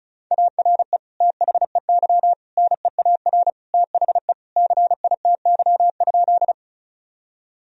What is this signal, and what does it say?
Morse code, heard as a keyed tone, about 36 wpm, 700 Hz. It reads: ARE THEY NEAR THE CITY?
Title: Sentences from Top 1000 Words - Review Part 1 - 35wpm